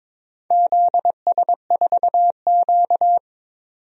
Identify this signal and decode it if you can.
Morse code, heard as a keyed tone, about 22 wpm, 700 Hz. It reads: ZS4Q
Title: DXCC-100 Call Signs - 4 Characters - 22wpm